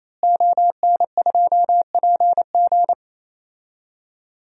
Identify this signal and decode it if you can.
Morse code, heard as a keyed tone, about 28 wpm, 700 Hz. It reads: ON2PG
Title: DXCC-100 Call Signs - 5 Characters - Repeated 3x - 28wpm